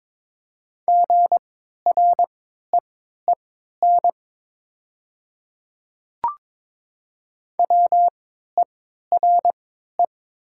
Morse code, heard as a keyed tone, about 22 wpm, spaced as for 13 wpm, 690 Hz. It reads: GREEN WERE
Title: 3x Character Spacing - Top 500 Words - 22wpm